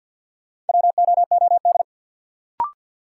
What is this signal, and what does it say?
Morse code, heard as a keyed tone, about 47 wpm, 700 Hz. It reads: WOOD
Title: Sentences from Top 1000 Words - Review Part 2 - New Word Drill - 50wpm